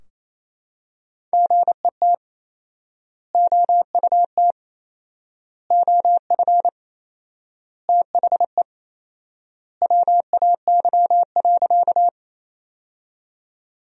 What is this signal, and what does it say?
Morse code, heard as a keyed tone, about 28 wpm, 700 Hz. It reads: GET OUT OF THE WAY.
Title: 4x Word-Spacing - Sentences from Top 200 Words - 28wpm